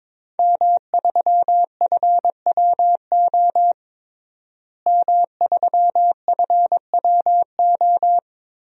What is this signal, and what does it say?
Morse code, heard as a keyed tone, about 22 wpm, 700 Hz. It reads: M3FWO M3FWO
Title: DXCC-100 Call Signs - 5 Characters - Repeated 3x - 22wpm